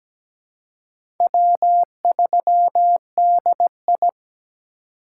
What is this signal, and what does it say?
Morse code, heard as a keyed tone, about 17 wpm, 700 Hz. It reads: W3DI
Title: USA Call Signs - 4 Characters - 3x Character Spacing - 17wpm